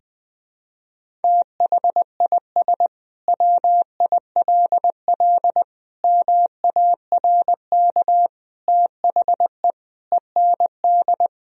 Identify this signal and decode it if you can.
Morse code, heard as a keyed tone, about 20 wpm, 700 Hz. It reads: THIS WILL MARK THE END
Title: Sentences from Top 1000 Words - Review Part 1 - 20wpm